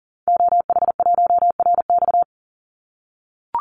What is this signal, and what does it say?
Morse code, heard as a keyed tone, about 40 wpm, 700 Hz. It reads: OH1RX
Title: Call Signs - 40wpm